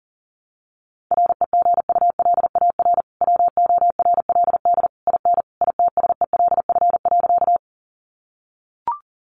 Morse code, heard as a keyed tone, about 40 wpm, 700 Hz. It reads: REGULAR WORLD IN ITSELF.
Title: Sentences from Top 900 Words 40wpm